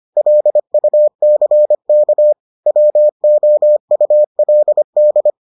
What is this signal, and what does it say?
Morse code, heard as a keyed tone, about 25 wpm, 600 Hz. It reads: LUCK WOULD